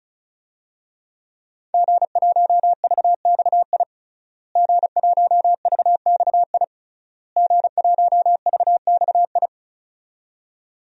Morse code, heard as a keyed tone, about 34 wpm, 700 Hz. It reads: G1VXI G1VXI G1VXI
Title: DXCC-100 Call Signs - 5 Characters - Repeated 3x - 35wpm